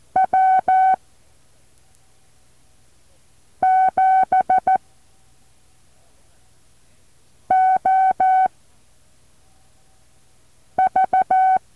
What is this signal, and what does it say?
Morse code, heard as a keyed tone, about 14 wpm, spaced as for 3 wpm, 760 Hz. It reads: W7OV
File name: intl morse code2